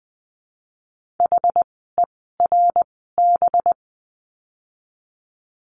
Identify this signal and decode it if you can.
Morse code, heard as a keyed tone, about 20 wpm, 700 Hz. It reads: H E R B
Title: 185 Lesson 30f - Letter b - Words 20wpm